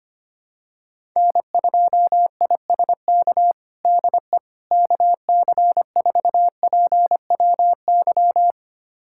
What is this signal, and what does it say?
Morse code, heard as a keyed tone, about 25 wpm, 700 Hz. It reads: N2ISK DE KC4PWY
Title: USA Callsign DE Callsign 25wpm